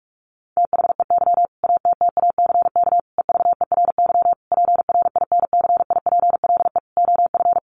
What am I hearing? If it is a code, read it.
Morse code, heard as a keyed tone, about 44 wpm, 700 Hz. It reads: THEY ATTACK EVERY PRINCIPLE OF
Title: Sentences from Top 1500 Words 45wpm